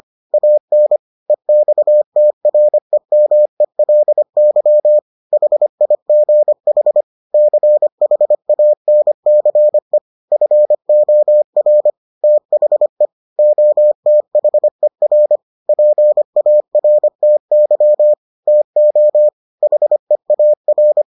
Morse code, heard as a keyed tone, about 25 wpm, 600 Hz. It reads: AN EXTREMELY HIGH CHANCE FOR THE OTHER PARTY TO HEAR